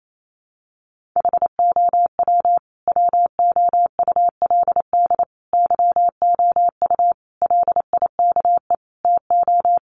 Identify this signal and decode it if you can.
Morse code, heard as a keyed tone, about 28 wpm, 700 Hz. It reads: HOW WOULD YOU LIKE TO
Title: Sentences from Top 100 Words 28wpm